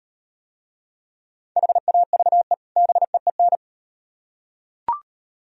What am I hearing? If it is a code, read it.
Morse code, heard as a keyed tone, about 38 wpm, 700 Hz. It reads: HAVE BEEN